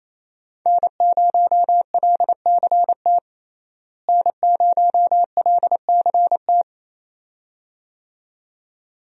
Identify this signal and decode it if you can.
Morse code, heard as a keyed tone, about 27 wpm, 700 Hz. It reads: N0LCT N0LCT
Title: USA Call Signs - 5 Characters - Repeated 2x - 28wpm